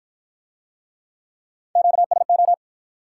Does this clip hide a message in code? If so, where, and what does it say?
Yes: Morse code from 1.8 to 2.5 s, heard as a keyed tone, about 53 wpm, 700 Hz. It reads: QSO